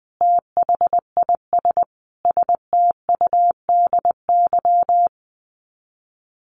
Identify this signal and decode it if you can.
Morse code, heard as a keyed tone, about 20 wpm, 700 Hz. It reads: THIS STUDY
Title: Sets of 5 Words 20wpm